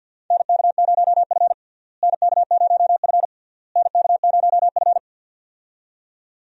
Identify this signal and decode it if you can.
Morse code, heard as a keyed tone, about 48 wpm, 700 Hz. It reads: NK0F NK0F NK0F
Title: USA Call Signs - 4 Characters - Repeated 3x - 50wpm